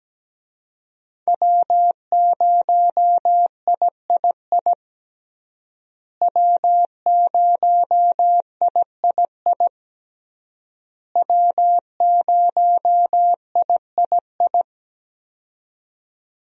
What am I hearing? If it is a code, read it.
Morse code, heard as a keyed tone, about 17 wpm, 700 Hz. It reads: W0III W0III W0III
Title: USA Call Signs - 5 Characters - Repeated 3x - 17wpm